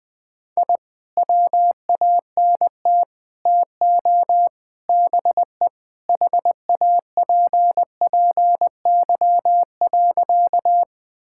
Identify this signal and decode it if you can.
Morse code, heard as a keyed tone, about 21 wpm, 700 Hz. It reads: I WANT TO BE HAPPY.
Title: Sentences with 4-Word Phrases - 20wpm